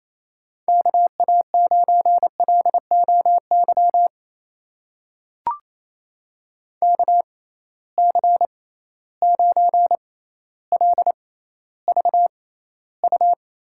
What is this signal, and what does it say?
Morse code, heard as a keyed tone, about 28 wpm, spaced as for 14 wpm, 700 Hz. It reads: KA9LOY KC9LVU
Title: USA Call Signs - 6 Characters - 6x Character Spacing - 28wpm